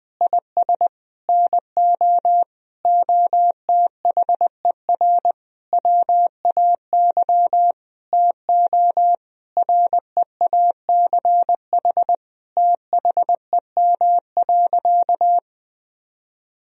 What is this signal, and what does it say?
Morse code, heard as a keyed tone, about 20 wpm, 700 Hz. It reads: IS NO OTHER WAY TO REACH THEM.
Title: Sentences from Top 1000 Words - Review Part 2 - 20wpm